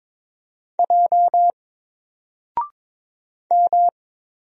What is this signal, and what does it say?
Morse code, heard as a keyed tone, about 22 wpm, 700 Hz. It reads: J M
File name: Single-Letter-Number-22wpm